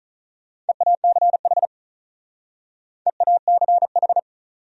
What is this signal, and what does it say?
Morse code, heard as a keyed tone, about 46 wpm, 700 Hz. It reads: EACH EACH